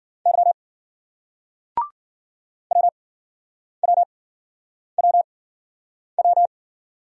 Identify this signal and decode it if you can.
Morse code, heard as a keyed tone, about 56 wpm, spaced as for 9 wpm, 700 Hz. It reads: Y WWWW